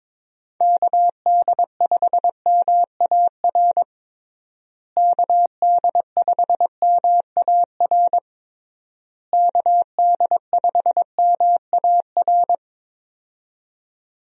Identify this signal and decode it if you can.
Morse code, heard as a keyed tone, about 22 wpm, 700 Hz. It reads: KD5MAR KD5MAR KD5MAR
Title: USA Call Signs - 6 Characters - Repeated 3x - 22wpm